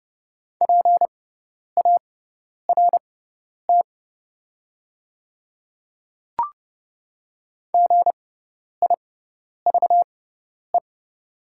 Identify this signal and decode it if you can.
Morse code, heard as a keyed tone, about 31 wpm, spaced as for 10 wpm, 700 Hz. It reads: PART GIVE
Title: 6x Character Spacing - Top 300 Words - 30wpm